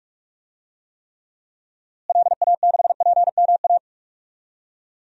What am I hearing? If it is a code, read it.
Morse code, heard as a keyed tone, about 46 wpm, 700 Hz. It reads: LA6PMA
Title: DXCC-100 Call Signs - 6 Characters - Repeated 2x - 45wpm